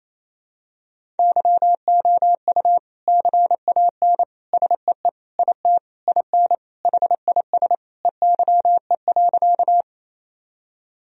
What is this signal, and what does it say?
Morse code, heard as a keyed tone, about 28 wpm, 700 Hz. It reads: YOU CAN SEE IT IN HIS EYE.
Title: Sentences from Top 1000 Words - Review Part 1 - 28wpm